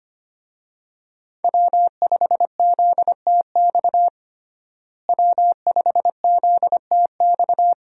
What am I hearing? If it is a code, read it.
Morse code, heard as a keyed tone, about 25 wpm, 700 Hz. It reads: W5ZTX W5ZTX
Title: USA Call Signs - 5 Characters - Repeated 2x - 25wpm